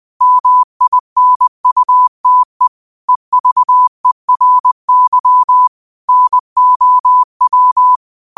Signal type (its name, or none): Morse code